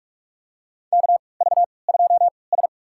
Morse code, heard as a keyed tone, about 44 wpm, spaced as for 25 wpm, 690 Hz. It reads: KV2S
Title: USA Call Signs - 4 Characters - 3x Character Spacing - 45wpm